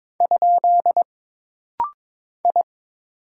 Morse code, heard as a keyed tone, about 22 wpm, 700 Hz. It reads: ? I
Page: This rapid-fire timing is geared to push your ICR (Instant Character Recognition) skills.